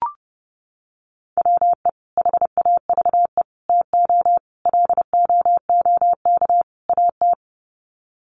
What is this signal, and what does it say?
Morse code, heard as a keyed tone, about 30 wpm, 700 Hz. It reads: WE HAVE TO LOOK AT
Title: Sets of 5 Words Encore2 30wpm